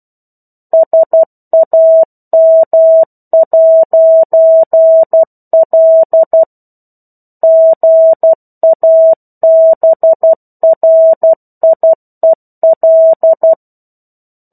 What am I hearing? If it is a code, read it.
Morse code, heard as a keyed tone, about 12 wpm, 650 Hz. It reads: SAM'L GABRIEL
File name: 01-Chapter-01 12 WPM